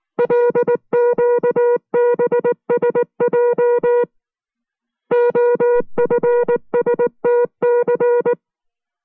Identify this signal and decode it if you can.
Morse code, heard as a keyed tone, about 19 wpm, 480 Hz. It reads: LQBSJ OFSTC